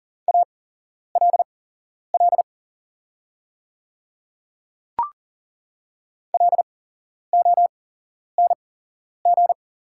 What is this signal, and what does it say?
Morse code, heard as a keyed tone, about 40 wpm, spaced as for 11 wpm, 690 Hz. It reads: ALL LONG
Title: 8x Character Spacing - Top 100 Words - 40wpm